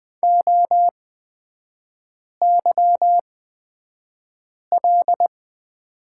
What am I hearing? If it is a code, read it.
Morse code, heard as a keyed tone, about 20 wpm, spaced as for 5 wpm, 700 Hz. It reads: OYL
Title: 243 Lesson 39e - Letter z - Three Characters 20wpm